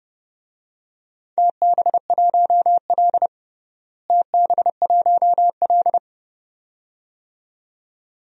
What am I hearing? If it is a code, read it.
Morse code, heard as a keyed tone, about 32 wpm, 700 Hz. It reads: TB1L TB1L
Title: DXCC-100 Call Signs - 4 Characters - Repeated 2x - 30wpm